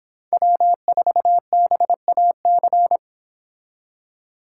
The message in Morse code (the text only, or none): W4BAC